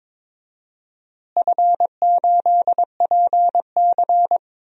Morse code, heard as a keyed tone, about 22 wpm, 700 Hz. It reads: F8PC
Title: DXCC-100 Call Signs - 4 Characters - Repeated 3x - 22wpm